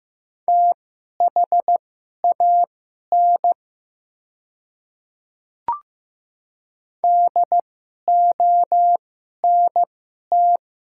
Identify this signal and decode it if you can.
Morse code, heard as a keyed tone, about 15 wpm, spaced as for 11 wpm, 700 Hz. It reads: THAN DONT
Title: 2x Character Spacing - Top 300 Words - 15wpm